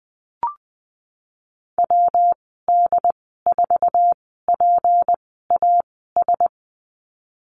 Morse code, heard as a keyed tone, about 20 wpm, spaced as for 14 wpm, 700 Hz. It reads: WD4PAS
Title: 179 Lesson 29g - Slash - Call Signs 20wpm